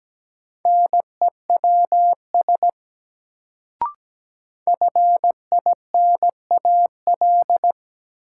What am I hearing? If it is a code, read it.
Morse code, heard as a keyed tone, about 17 wpm, 700 Hz. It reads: NEWS FINAL
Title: Sentences from Top 800 Words - New Word Drill - 17wpm